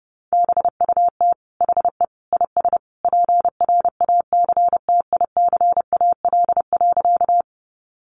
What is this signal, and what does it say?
Morse code, heard as a keyed tone, about 30 wpm, 700 Hz. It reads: BUT HE IS PRACTICAL.
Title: Sentences from Top 1400 Words 30wpm